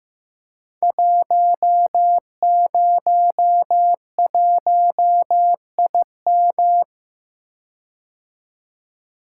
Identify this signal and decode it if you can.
Morse code, heard as a keyed tone, about 15 wpm, 700 Hz. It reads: 101IM
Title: Code Groups - 5 Letter:Numbers - 15wpm